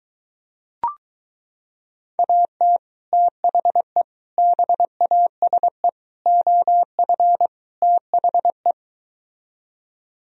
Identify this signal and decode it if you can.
Morse code, heard as a keyed tone, about 23 wpm, 700 Hz. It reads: AT THE BASE OF THE